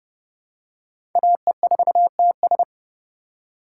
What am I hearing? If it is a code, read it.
Morse code, heard as a keyed tone, about 32 wpm, 700 Hz. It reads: AE4TS
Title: USA Call Signs - 5 Characters - 2x Character Spacing - 30wpm